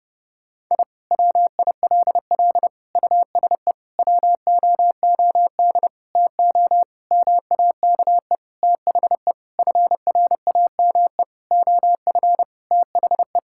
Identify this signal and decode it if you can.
Morse code, heard as a keyed tone, about 30 wpm, 700 Hz. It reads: I WILL USE WOOD TO MAKE THE FRAME OF THE
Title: Sentences from Top 1000 Words - Review Part 6 - 30wpm